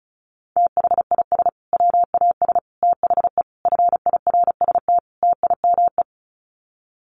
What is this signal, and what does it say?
Morse code, heard as a keyed tone, about 34 wpm, 700 Hz. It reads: THIS WAS THE FIRST TIME
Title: Sets of 5 Words Encore 35wpm